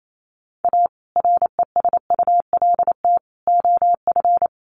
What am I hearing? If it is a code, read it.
Morse code, heard as a keyed tone, about 28 wpm, 700 Hz. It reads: A RESULT OF
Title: Sets of 3 Words 28wpm